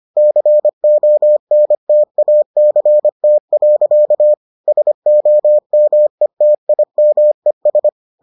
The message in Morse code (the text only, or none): CONTACT. SOMETIMES